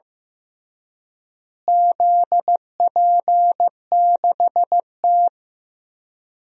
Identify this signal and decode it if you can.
Morse code, heard as a keyed tone, about 15 wpm, 700 Hz. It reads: ZP6T
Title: DXCC-100 Call Signs - 4 Characters - Repeated 3x - 15wpm